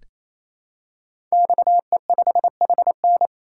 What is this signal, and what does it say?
Morse code, heard as a keyed tone, about 29 wpm, 700 Hz. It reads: XE5HN
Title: Code Groups - 5 Letter:Numbers - 28wpm